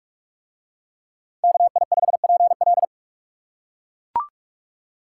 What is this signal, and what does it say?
Morse code, heard as a keyed tone, about 40 wpm, 700 Hz. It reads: KI5PL